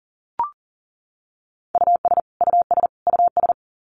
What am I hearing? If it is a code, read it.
Morse code, heard as a keyed tone, about 40 wpm, 700 Hz. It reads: US US US
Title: Top 200 Words Repeated 40wpm